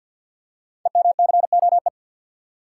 Morse code, heard as a keyed tone, about 51 wpm, 690 Hz. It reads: EMCOE
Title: Code Groups - 5 Letter:Numbers - 50wpm